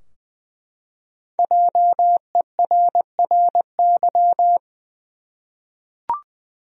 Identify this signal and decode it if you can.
Morse code, heard as a keyed tone, about 21 wpm, 700 Hz. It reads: JERRY
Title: USA Common Names - 20wpm